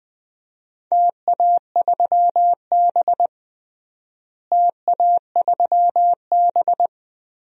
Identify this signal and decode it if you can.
Morse code, heard as a keyed tone, about 21 wpm, 700 Hz. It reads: TA3B TA3B
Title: DXCC-100 Call Signs - 4 Characters - Repeated 3x - 20wpm